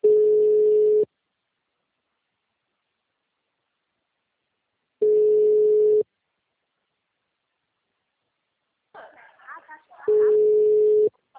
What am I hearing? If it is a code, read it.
Ringback tone (Europe, 425 Hz), with speech.